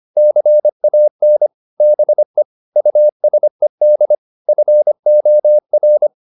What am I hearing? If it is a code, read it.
Morse code, heard as a keyed tone, about 25 wpm, 600 Hz. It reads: CAN BE USED FOR